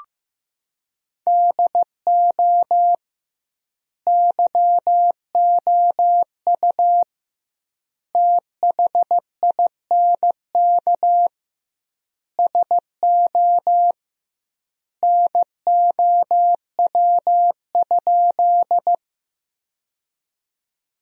Morse code, heard as a keyed tone, about 15 wpm, 700 Hz. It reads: DO YOU THINK SO NOW?
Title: 2x Word-Spacing - Sentences from Top 200 Words - 15wpm